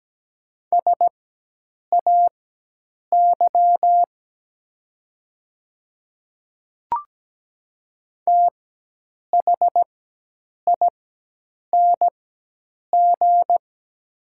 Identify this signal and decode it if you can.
Morse code, heard as a keyed tone, about 17 wpm, spaced as for 8 wpm, 690 Hz. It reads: SAY THING
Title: 4x Character Spacing - Top 300 Words - 17wpm